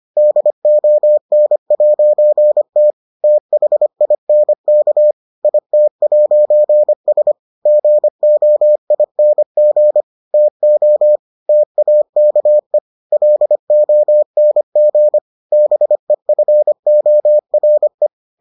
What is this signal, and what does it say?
Morse code, heard as a keyed tone, about 25 wpm, 600 Hz. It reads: DON'T THINK IT'S GOING TO TAKE LONG BEFORE